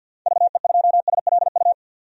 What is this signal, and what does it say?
Morse code, heard as a keyed tone, about 52 wpm, 700 Hz. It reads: VE2SLU
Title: DXCC-100 Call Signs - 6 Characters - 50wpm